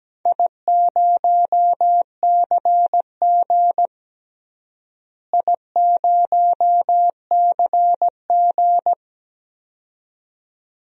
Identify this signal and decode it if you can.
Morse code, heard as a keyed tone, about 17 wpm, 700 Hz. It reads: I0CG I0CG
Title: DXCC-100 Call Signs - 4 Characters - Repeated 2x - 17wpm